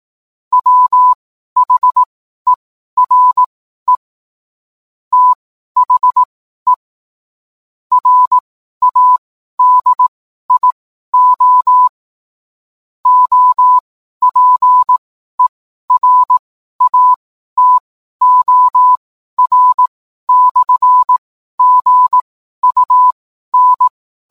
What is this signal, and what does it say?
Morse code, heard as a keyed tone, about 18 wpm, spaced as for 12 wpm, 1000 Hz. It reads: WHERE THE RADIO OPERATOR/GUN
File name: cwblog / cw mp3 directory / mem morse 12